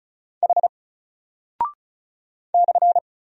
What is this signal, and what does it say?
Morse code, heard as a keyed tone, about 36 wpm, 700 Hz. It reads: H /
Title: Single Character (Letter:Number:Punctuation:Prosign) Alt 35wpm